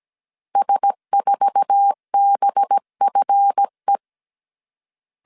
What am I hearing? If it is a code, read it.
Morse code, heard as a keyed tone, about 17 wpm, 800 Hz. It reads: S4BFE
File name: CW / DET / 1 9 1500C S C 16wpm 2909